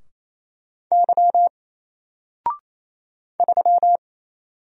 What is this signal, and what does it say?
Morse code, heard as a keyed tone, about 28 wpm, 700 Hz. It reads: Y 3
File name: Single-Letter-Number-28wpm